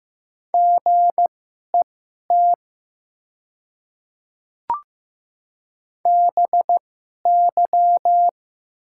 Morse code, heard as a keyed tone, about 15 wpm, spaced as for 11 wpm, 700 Hz. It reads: GET BY
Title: 2x Character Spacing - Top 500 Words - 15wpm